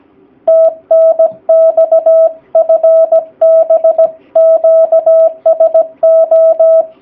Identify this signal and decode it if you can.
Morse code, heard as a keyed tone, about 17 wpm, 650 Hz. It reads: TNX FB QSO
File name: jh8cbh / 2019cw / sp16- w